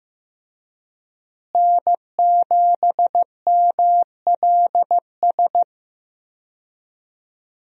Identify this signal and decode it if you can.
Morse code, heard as a keyed tone, about 15 wpm, 700 Hz. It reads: N7MLS
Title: USA Call Signs - 5 Characters - Repeated 2x - 15wpm